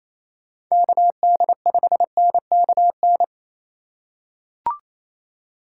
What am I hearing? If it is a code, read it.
Morse code, heard as a keyed tone, about 28 wpm, 700 Hz. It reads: KD5NKN
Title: USA Call Signs - 6 Characters - 2x Character Spacing - 28wpm